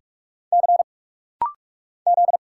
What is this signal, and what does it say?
Morse code, heard as a keyed tone, about 46 wpm, 700 Hz. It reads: C Z
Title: Single Letter-Number Alt - Rapid-Fire 45wpm